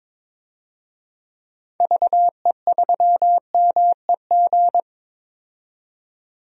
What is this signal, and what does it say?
Morse code, heard as a keyed tone, about 22 wpm, 700 Hz. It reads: VE3MEG
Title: DXCC-100 Call Signs - 6 Characters - 22wpm